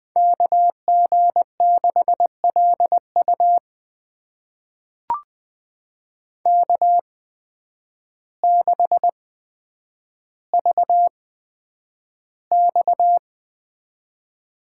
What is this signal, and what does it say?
Morse code, heard as a keyed tone, about 20 wpm, spaced as for 9 wpm, 700 Hz. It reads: KG6LU K6VX
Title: USA Call Signs - 5 Characters - 8x Character Spacing - 20wpm